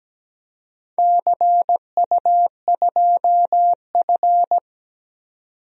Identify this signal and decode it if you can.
Morse code, heard as a keyed tone, about 17 wpm, 700 Hz. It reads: CU2F
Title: DXCC-100 Call Signs - 4 Characters - 17wpm